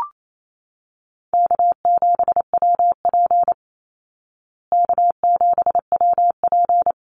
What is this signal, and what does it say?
Morse code, heard as a keyed tone, about 28 wpm, 700 Hz. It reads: K7WP K7WP
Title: Easy Call Signs Repeated 2x - 28wpm